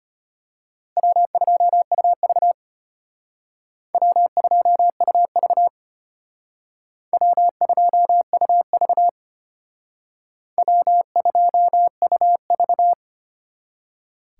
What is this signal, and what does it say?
Morse code, heard as a keyed tone, about 32 wpm, 700 Hz. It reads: W2UV W2UV W2UV W2UV